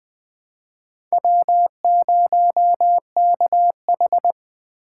Morse code, heard as a keyed tone, about 20 wpm, 700 Hz. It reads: W0KH
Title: USA Call Signs - 2x Character Spacing - 20wpm